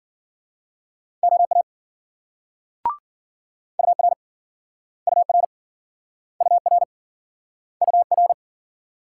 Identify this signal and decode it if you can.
Morse code, heard as a keyed tone, about 58 wpm, 700 Hz. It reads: OA UR UR UR UR